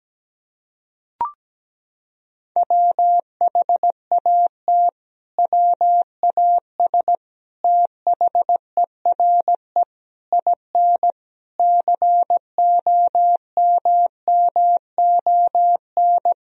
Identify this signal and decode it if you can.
Morse code, heard as a keyed tone, about 17 wpm, 700 Hz. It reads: WHAT WAS THERE IN COMMON
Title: Sentences from Top 1000 Words - Review Part 3 - 17wpm